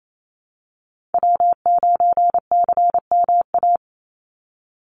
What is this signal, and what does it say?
Morse code, heard as a keyed tone, about 28 wpm, 700 Hz. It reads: W9CMA
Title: USA Call Signs 28wpm